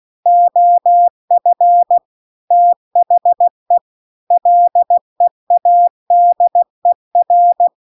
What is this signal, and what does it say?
Morse code, heard as a keyed tone, about 16 wpm, 700 Hz. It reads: OF THE LEADER
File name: CW-The-Big-Twoheaded-River-2-16WPM